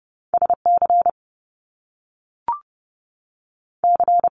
Morse code, heard as a keyed tone, about 30 wpm, 700 Hz. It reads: S C C
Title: Top Two Letter Combinations in Words 30wpm